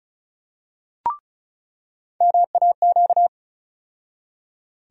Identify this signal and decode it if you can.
Morse code, heard as a keyed tone, about 34 wpm, 700 Hz. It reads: MAQ
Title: USA Call Sign Postfixes - 3 Letters 35wpm